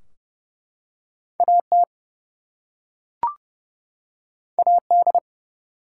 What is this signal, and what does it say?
Morse code, heard as a keyed tone, about 31 wpm, 700 Hz. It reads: AT AD
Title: Two Letter Words - 30wpm